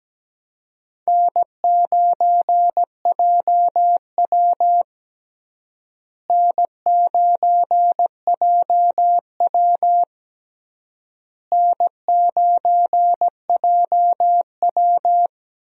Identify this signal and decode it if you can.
Morse code, heard as a keyed tone, about 17 wpm, 700 Hz. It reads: N9JW N9JW N9JW
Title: USA Call Signs - 4 Characters - Repeated 3x - 17wpm